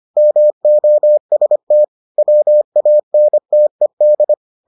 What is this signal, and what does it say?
Morse code, heard as a keyed tone, about 25 wpm, 600 Hz. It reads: MOST WANTED